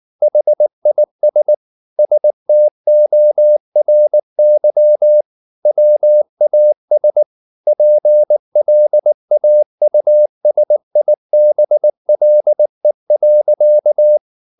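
Morse code, heard as a keyed tone, about 19 wpm, 600 Hz. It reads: HIS STORY WAS PLAUSIBLE.
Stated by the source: Morse code at 19 wpm from CW med Gnister, reading Ken Follett's novel Never in English.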